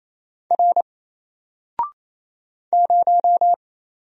Morse code, heard as a keyed tone, about 29 wpm, 700 Hz. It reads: R 0
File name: Single-Letter-Number-28wpm